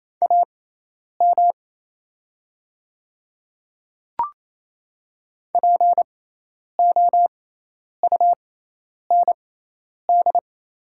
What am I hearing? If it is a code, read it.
Morse code, heard as a keyed tone, about 28 wpm, spaced as for 10 wpm, 690 Hz. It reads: AM POUND
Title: 6x Character Spacing - Top 500 Words - 28wpm